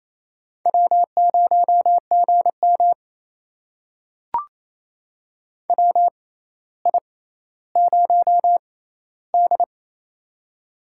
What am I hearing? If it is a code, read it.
Morse code, heard as a keyed tone, about 28 wpm, spaced as for 14 wpm, 700 Hz. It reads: W0GM WI0D
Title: USA Call Signs - 4 Characters - 6x Character Spacing - 28wpm